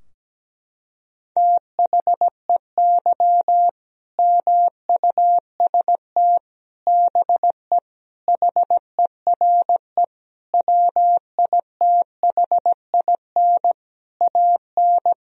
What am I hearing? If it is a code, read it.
Morse code, heard as a keyed tone, about 17 wpm, 700 Hz. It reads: THEY MUST BE HERE WITHIN AN
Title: Sentences from Top 1000 Words - Review Part 4 - 17wpm